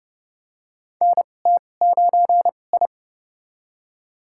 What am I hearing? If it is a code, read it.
Morse code, heard as a keyed tone, about 29 wpm, spaced as for 21 wpm, 700 Hz. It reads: NT9I
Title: USA Call Signs - 2x Character Spacing - 30wpm